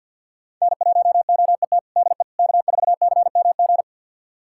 Morse code, heard as a keyed tone, about 45 wpm, 700 Hz. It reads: N1OET DE K4CMG
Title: USA Callsign DE Callsign - 5 Characters - 50wpm